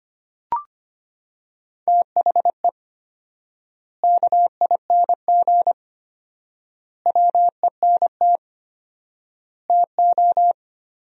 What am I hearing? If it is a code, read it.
Morse code, heard as a keyed tone, about 25 wpm, 700 Hz. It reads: THE KING WENT TO
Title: 4x Word-Spacing - Sentences from Top 400 Words - 25wpm